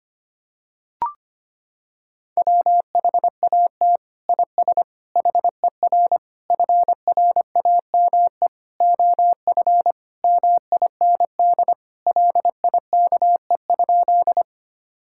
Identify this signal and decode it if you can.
Morse code, heard as a keyed tone, about 25 wpm, 700 Hz. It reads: WHAT IS HER FRAME OF MIND LIKE?
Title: Sentences from Top 1000 Words - Review Part 6 - 25wpm